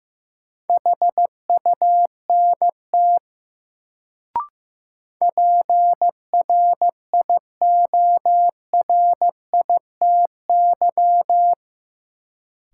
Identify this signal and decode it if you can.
Morse code, heard as a keyed tone, about 16 wpm, 700 Hz. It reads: HUNT PRIORITY
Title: Sentences from Top 1600 Words - New Word Drill - 15wpm